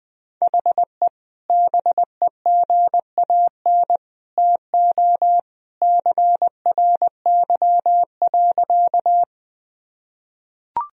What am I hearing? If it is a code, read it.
Morse code, heard as a keyed tone, about 20 wpm, 700 Hz. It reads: HE BEGAN TO CRY.
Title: Sentences from Top 1000 Words - Review Part 2 - 20wpm